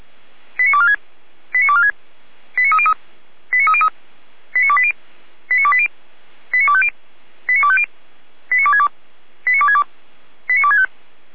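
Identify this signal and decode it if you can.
Five-tone selective call (ZVEI), heard as quick runs of five tones, tones of about 70 ms. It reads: 89246 89246 89303 89303 89280 89280 89250 89250 89262 89262 89265
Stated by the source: ZVEI five-tone call received by SDR, Feuerwehr Orlamünde fire brigade.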